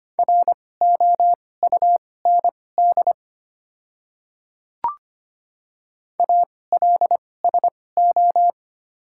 Morse code, heard as a keyed tone, about 25 wpm, spaced as for 18 wpm, 700 Hz. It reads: ROUND ALSO
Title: 2x Character Spacing - Top 500 Words - 25wpm